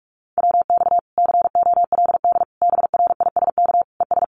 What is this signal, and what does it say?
Morse code, heard as a keyed tone, about 44 wpm, 700 Hz. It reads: WX COLD BRISK ES
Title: Common QSO Phrases Encore 45wpm